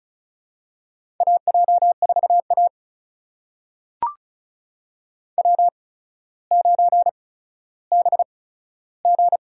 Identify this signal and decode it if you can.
Morse code, heard as a keyed tone, about 36 wpm, spaced as for 15 wpm, 700 Hz. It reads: AJ4A W9BG
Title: USA Call Signs - 4 Characters - 8x Character Spacing - 35wpm